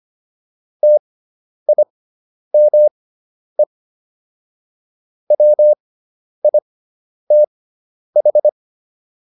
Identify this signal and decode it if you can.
Morse code, heard as a keyed tone, about 25 wpm, spaced as for 10 wpm, 600 Hz. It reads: TIME WITH